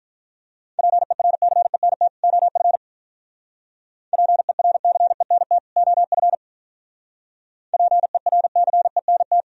Morse code, heard as a keyed tone, about 49 wpm, 700 Hz. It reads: PERCENT OF PERCENT OF PERCENT